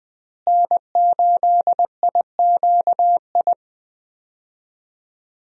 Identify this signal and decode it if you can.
Morse code, heard as a keyed tone, about 21 wpm, 700 Hz. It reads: N8IQI
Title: USA Call Signs - 5 Characters - Repeated 2x - 20wpm